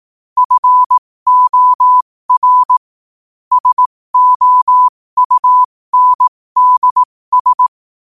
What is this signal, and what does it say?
Morse code, heard as a keyed tone, about 18 wpm, spaced as for 15 wpm, 1000 Hz. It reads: FOR SOUNDS
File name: qotdfiles / 15wpm